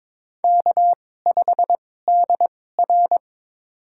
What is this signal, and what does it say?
Morse code, heard as a keyed tone, about 22 wpm, spaced as for 16 wpm, 700 Hz. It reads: K5DR
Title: USA Call Signs - 4 Characters - 2x Character Spacing - 22wpm